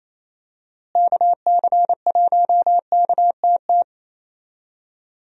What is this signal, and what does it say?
Morse code, heard as a keyed tone, about 28 wpm, 700 Hz. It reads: KC1KTT
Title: USA Call Signs - 6 Characters - Repeated 3x - 28wpm